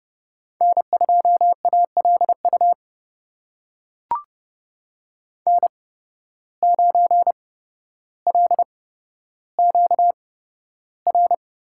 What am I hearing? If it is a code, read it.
Morse code, heard as a keyed tone, about 29 wpm, spaced as for 13 wpm, 700 Hz. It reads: N2ALU N9LQR
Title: USA Call Signs - 5 Characters - 8x Character Spacing - 30wpm